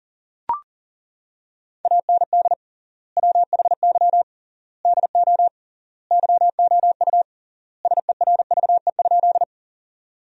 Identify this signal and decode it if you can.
Morse code, heard as a keyed tone, about 40 wpm, 700 Hz. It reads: AND WHY DO YOU SERVE?
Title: 3x Word-Spacing - Sentences from Top 400 Words - 40wpm